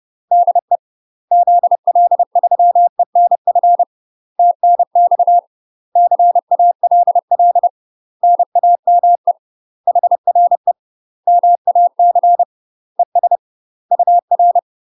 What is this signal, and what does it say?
Morse code, heard as a keyed tone, about 30 wpm, 700 Hz. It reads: DE ZL3ENF TNX CALL NAME HRE MAC ES UR
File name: UK MASTER QSOs TRAINING / 30WPM / UK MASTER QSOs 0021